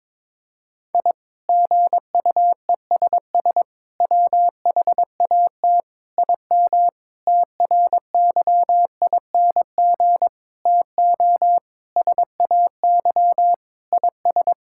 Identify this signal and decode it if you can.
Morse code, heard as a keyed tone, about 22 wpm, 700 Hz. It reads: I GUESS WHAT IM TRYING TO SAY IS
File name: practice / 22wpm / Sets of 8 Words - 22wpm